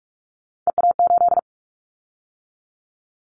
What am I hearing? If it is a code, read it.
Morse code, heard as a keyed tone, about 44 wpm, 700 Hz. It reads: EA8
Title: Callsign Prefixes for Top 100 DXCC - 45wpm